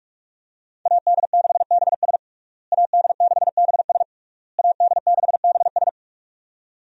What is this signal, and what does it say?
Morse code, heard as a keyed tone, about 46 wpm, 700 Hz. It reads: AD6BS AD6BS AD6BS
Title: USA Call Signs - 5 Characters - Repeated 3x - 45wpm